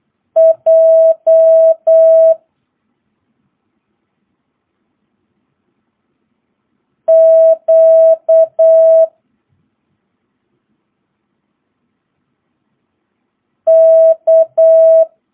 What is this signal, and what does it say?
Morse code, heard as a keyed tone, about 8 wpm, 650 Hz. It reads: J Q K
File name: jh8cbh / 2022cw